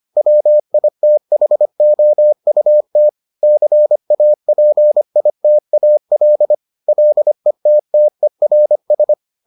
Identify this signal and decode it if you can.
Morse code, heard as a keyed tone, about 25 wpm, 600 Hz. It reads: WITHOUT CAPITAL LETTERS